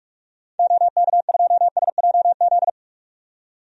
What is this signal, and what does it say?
Morse code, heard as a keyed tone, about 46 wpm, 700 Hz. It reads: OK2SJZ
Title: DXCC-100 Call Signs - 6 Characters - 45wpm